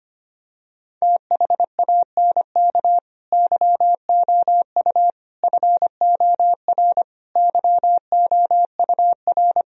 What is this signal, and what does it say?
Morse code, heard as a keyed tone, about 25 wpm, 700 Hz. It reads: THANK YOU FOR YOUR
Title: Sentences from Top 1000 Words - Review Part 3 - 25wpm